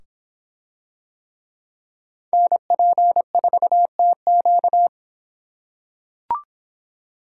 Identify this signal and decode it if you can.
Morse code, heard as a keyed tone, about 26 wpm, 700 Hz. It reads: NP4TQ